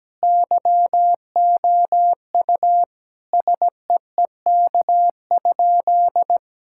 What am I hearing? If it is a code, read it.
Morse code, heard as a keyed tone, about 17 wpm, 700 Hz. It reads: YOU SEEK?
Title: Sentences from Top 1600 Words 17wpm